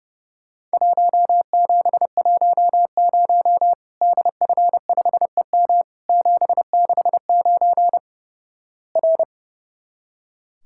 Morse code, heard as a keyed tone, about 32 wpm, 700 Hz. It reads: 1710 DF5EM 769 R
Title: Worked All Europe DX Contest - QTC Exchange - 30wpm